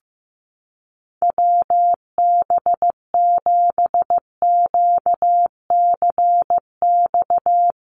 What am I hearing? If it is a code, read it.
Morse code, heard as a keyed tone, about 15 wpm, 700 Hz. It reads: WB7QCX
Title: USA Call Signs Repeated 2x 15wpm